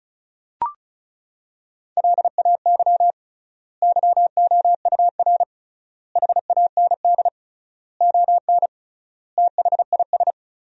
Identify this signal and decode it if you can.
Morse code, heard as a keyed tone, about 34 wpm, 700 Hz. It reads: LAY YOUR HAND ON THIS
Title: 3x Word-Spacing - Sentences from Top 400 Words - 35wpm